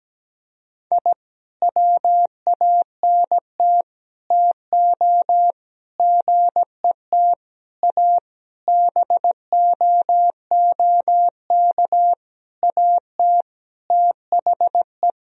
Morse code, heard as a keyed tone, about 17 wpm, 700 Hz. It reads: I WANT TO GET A BOOK AT THE
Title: Sentences from Top 1000 Words - Review Part 1 - 17wpm